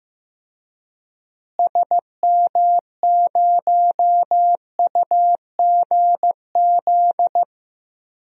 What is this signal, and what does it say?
Morse code, heard as a keyed tone, about 15 wpm, 700 Hz. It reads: SM0UGZ
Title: DXCC-100 Call Signs - 6 Characters - Repeated 3x - 15wpm